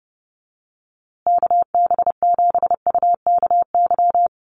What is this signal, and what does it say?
Morse code, heard as a keyed tone, about 30 wpm, 700 Hz. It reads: KB7UKY
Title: USA Call Signs Repeated 2x 30wpm